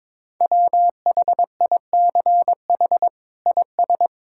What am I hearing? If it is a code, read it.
Morse code, heard as a keyed tone, about 22 wpm, 700 Hz. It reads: WHICH IS
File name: practice / 22wpm / Sets of 8 Words - 22wpm